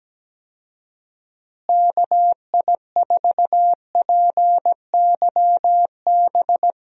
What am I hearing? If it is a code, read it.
Morse code, heard as a keyed tone, about 17 wpm, 700 Hz. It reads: KI4PYB
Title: USA Call Signs - 6 Characters - Repeated 3x - 17wpm